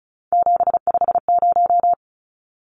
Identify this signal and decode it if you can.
Morse code, heard as a keyed tone, about 34 wpm, 700 Hz. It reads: 750
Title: Numbers - 3-Digit 35wpm